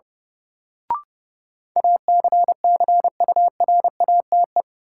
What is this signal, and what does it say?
Morse code, heard as a keyed tone, about 30 wpm, 700 Hz. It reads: ACCURATE
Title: Sentences from Top 1200 Words - New Word Drill Alt - 30wpm